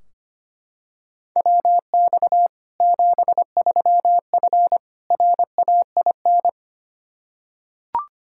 Morse code, heard as a keyed tone, about 25 wpm, 700 Hz. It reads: WX 73F RAIN
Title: Practice QSO - Exchange 2 - Info - 25wpm